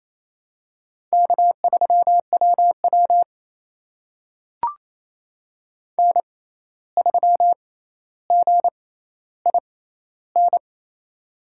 Morse code, heard as a keyed tone, about 28 wpm, spaced as for 13 wpm, 700 Hz. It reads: K3WW N3GIN
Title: USA Call Signs - 6x Character Spacing - 28wpm